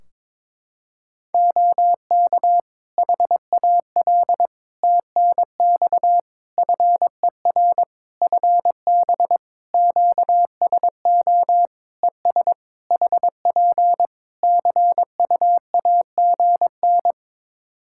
Morse code, heard as a keyed tone, about 22 wpm, 700 Hz. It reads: OK HAL TNX FER FB QSO ES HP CUAGN
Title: Practice QSO - Exchange 4 - Ending - 22wpm